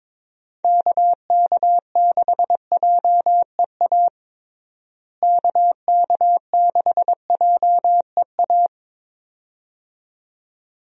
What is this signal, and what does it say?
Morse code, heard as a keyed tone, about 22 wpm, 700 Hz. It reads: KK6JEA KK6JEA
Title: USA Call Signs - 6 Characters - Repeated 2x - 22wpm